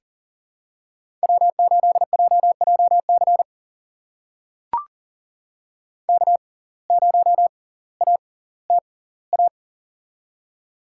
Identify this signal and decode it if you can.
Morse code, heard as a keyed tone, about 40 wpm, spaced as for 20 wpm, 700 Hz. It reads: W8JJC K0ATA
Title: USA Call Signs - 5 Characters - 6x Character Spacing - 40wpm